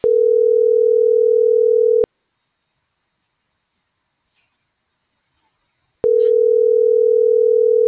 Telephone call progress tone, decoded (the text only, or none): ringback tone